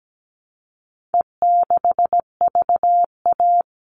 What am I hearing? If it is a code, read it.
Morse code, heard as a keyed tone, about 17 wpm, 700 Hz. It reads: E6VA
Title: All Cut Numbers - 4-Digit 17wpm